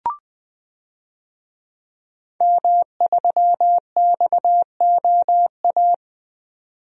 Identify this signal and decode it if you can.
Morse code, heard as a keyed tone, about 20 wpm, 700 Hz. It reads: M3XOA
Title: DXCC-100 Call Signs - 5 Characters - 20wpm